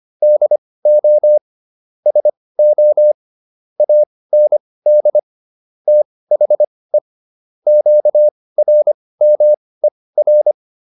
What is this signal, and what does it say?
Morse code, heard as a keyed tone, about 25 wpm, spaced as for 18 wpm, 600 Hz. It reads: DO SO AND THE QRMER